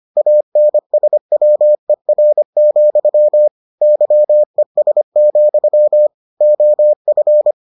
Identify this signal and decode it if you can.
Morse code, heard as a keyed tone, about 25 wpm, 600 Hz. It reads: ANSWER, YES, OF